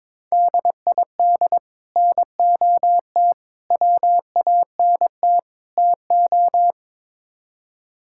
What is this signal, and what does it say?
Morse code, heard as a keyed tone, about 22 wpm, 700 Hz. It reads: DID NOT WANT TO
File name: practice / 22wpm / Build A Phrase - 5-Word Phrases - 22wpm